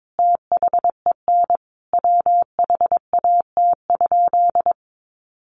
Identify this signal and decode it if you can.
Morse code, heard as a keyed tone, about 22 wpm, 700 Hz. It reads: THEN WHAT?
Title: Sentences from Top 1500 Words 22wpm